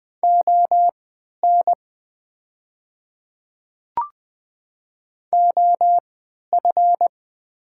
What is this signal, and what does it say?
Morse code, heard as a keyed tone, about 20 wpm, spaced as for 11 wpm, 700 Hz. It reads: ON OF
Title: 3x Character Spacing - Top 200 Words - 20wpm